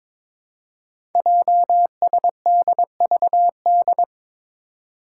Morse code, heard as a keyed tone, about 22 wpm, 700 Hz. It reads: JSDVD
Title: Code Groups - 5 Letters - 22wpm